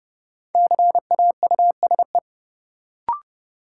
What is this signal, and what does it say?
Morse code, heard as a keyed tone, about 30 wpm, 700 Hz. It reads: CAUSE
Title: Sentences from Top 200 Words - New Word Drill - 30wpm